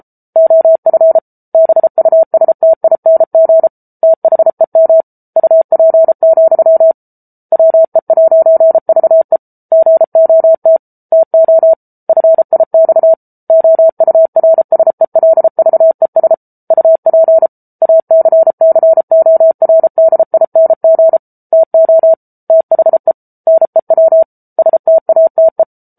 Morse code, heard as a keyed tone, about 34 wpm, 650 Hz. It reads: OF BUSTING THEM UP, WE'VE GOT TO FIX OURSELVES UP ACCORDING TO THE NEW STATE